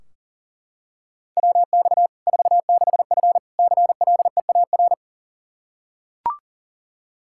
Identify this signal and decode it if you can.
Morse code, heard as a keyed tone, about 37 wpm, 700 Hz. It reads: WX 46F CLEAR
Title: Practice QSO - Exchange 2 - Info - 40wpm